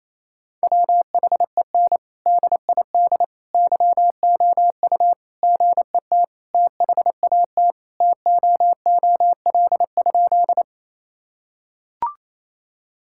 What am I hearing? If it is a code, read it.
Morse code, heard as a keyed tone, about 27 wpm, 700 Hz. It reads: WHEN DID YOU GET THAT TOOL?
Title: Sentences from Top 1000 Words - Review Part 6 - 28wpm